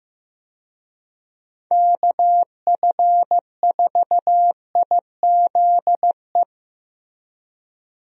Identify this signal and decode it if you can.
Morse code, heard as a keyed tone, about 15 wpm, 700 Hz. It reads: KF4IZE
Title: USA Call Signs - 6 Characters - Repeated 2x - 15wpm